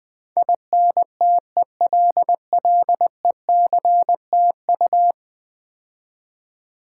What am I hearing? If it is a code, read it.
Morse code, heard as a keyed tone, about 20 wpm, 700 Hz. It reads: INTELLECTU
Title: Build A Word - 5 Letters At A Time - 12-Letter Words - 20wpm